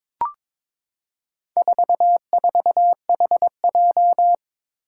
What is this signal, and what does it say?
Morse code, heard as a keyed tone, about 22 wpm, 700 Hz. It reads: 44HJ
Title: Four Challenging Letter-Number Combinations 22wpm